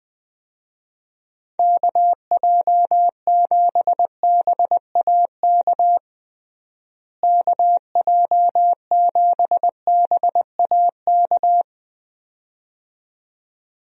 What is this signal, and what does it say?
Morse code, heard as a keyed tone, about 20 wpm, 700 Hz. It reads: KJ7BAK KJ7BAK
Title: USA Call Signs - 6 Characters - Repeated 2x - 20wpm